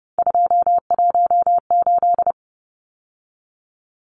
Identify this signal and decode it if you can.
Morse code, heard as a keyed tone, about 30 wpm, 700 Hz. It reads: 218
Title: Numbers - 3-Digit 30wpm